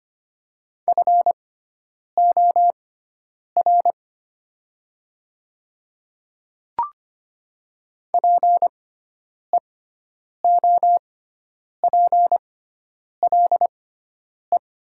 Morse code, heard as a keyed tone, about 25 wpm, spaced as for 9 wpm, 700 Hz. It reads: FOR PEOPLE
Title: 6x Character Spacing - Top 200 Words - 25wpm